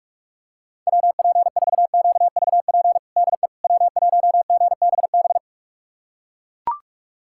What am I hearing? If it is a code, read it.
Morse code, heard as a keyed tone, about 42 wpm, 700 Hz. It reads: WP4QVP DE W1GBB
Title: USA Callsign DE Callsign 45wpm